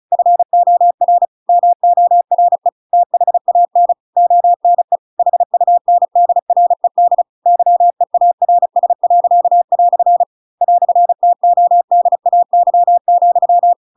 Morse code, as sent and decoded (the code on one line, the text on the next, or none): ..-. --- .-. / -- --- .-. / - .... .- -. / --- -. / .... ..- -. -.. .-. -.. / -.-- . .- .-. ... .-.-.- .-..-. / .-..-. - --- -.. .- -.-- --..--
FOR MORE THAN ONE HUNDRED YEARS." "TODAY,